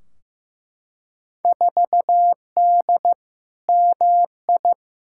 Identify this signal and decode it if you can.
Morse code, heard as a keyed tone, about 15 wpm, 700 Hz. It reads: 4D MI
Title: Field Day Exchanges Encore - 15wpm